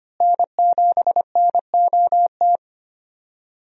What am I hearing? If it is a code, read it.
Morse code, heard as a keyed tone, about 25 wpm, 700 Hz. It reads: N7NOT
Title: USA Call Signs - 5 Characters - Repeated 3x - 25wpm